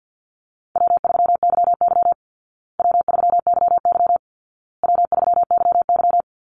Morse code, heard as a keyed tone, about 50 wpm, 700 Hz. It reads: W3YY W3YY W3YY
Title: Call Signs Repeated 2x - 50wpm